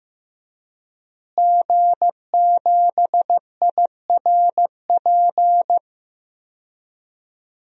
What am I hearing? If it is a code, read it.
Morse code, heard as a keyed tone, about 15 wpm, 700 Hz. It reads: G7IRP
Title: DXCC-100 Call Signs - 5 Characters - Repeated 3x - 15wpm